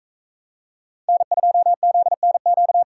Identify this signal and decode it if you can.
Morse code, heard as a keyed tone, about 44 wpm, 700 Hz. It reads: N2ZNQ